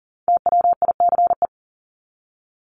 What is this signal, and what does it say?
Morse code, heard as a keyed tone, about 40 wpm, 700 Hz. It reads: TWICE
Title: Top 1501-1750 Words 40wpm